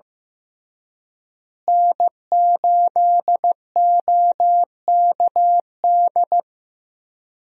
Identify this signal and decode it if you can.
Morse code, heard as a keyed tone, about 15 wpm, 700 Hz. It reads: N8OKD
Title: USA Call Signs - 5 Characters - Repeated 2x - 15wpm